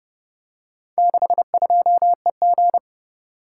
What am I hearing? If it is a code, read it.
Morse code, heard as a keyed tone, about 29 wpm, 700 Hz. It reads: 62EG